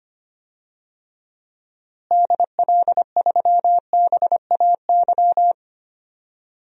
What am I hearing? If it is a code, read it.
Morse code, heard as a keyed tone, about 25 wpm, 700 Hz. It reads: DL3BAY
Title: DXCC-100 Call Signs - 6 Characters - Repeated 3x - 25wpm